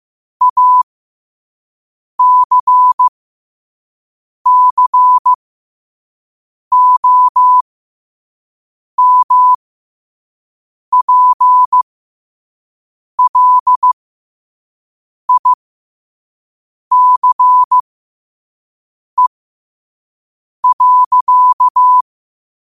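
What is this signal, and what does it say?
Morse code, heard as a keyed tone, about 15 wpm, spaced as for 5 wpm, 1000 Hz. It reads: ACCOMPLICE.
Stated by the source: 5 wpm Morse code practice recording.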